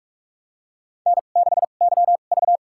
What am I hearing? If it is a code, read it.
Morse code, heard as a keyed tone, about 45 wpm, spaced as for 32 wpm, 700 Hz. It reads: N6YV
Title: USA Call Signs - 4 Characters - 2x Character Spacing - 45wpm